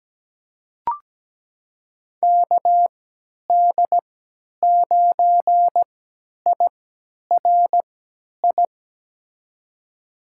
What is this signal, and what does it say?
Morse code, heard as a keyed tone, about 17 wpm, spaced as for 10 wpm, 700 Hz. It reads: KD9IRI
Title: USA Call Signs - 6 Characters - 3x Character Spacing - 17wpm